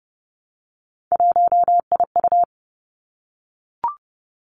Morse code, heard as a keyed tone, about 30 wpm, 700 Hz. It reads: 1IU
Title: Three Challenging Letter-Number Combinations 30wpm